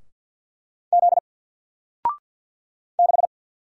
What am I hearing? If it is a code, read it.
Morse code, heard as a keyed tone, about 48 wpm, 700 Hz. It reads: Z 6
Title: Single Letter-Number Alt 50wpm